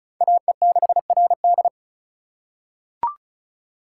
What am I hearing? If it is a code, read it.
Morse code, heard as a keyed tone, about 36 wpm, 700 Hz. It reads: AE6RD
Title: USA Call Signs - 5 Characters - 2x Character Spacing - 35wpm